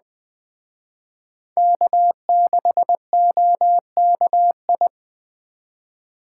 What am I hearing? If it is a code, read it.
Morse code, heard as a keyed tone, about 22 wpm, 700 Hz. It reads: K6OKI